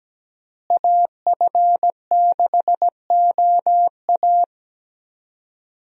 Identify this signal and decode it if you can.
Morse code, heard as a keyed tone, about 17 wpm, 700 Hz. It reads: AF6OA
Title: USA Call Signs - 5 Characters - Repeated 2x - 17wpm